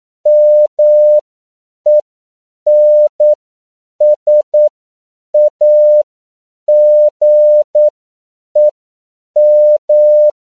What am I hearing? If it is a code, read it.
Morse code, heard as a keyed tone, about 9 wpm, spaced as for 7 wpm, 600 Hz. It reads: MENSAGEM
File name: CW / 10PPM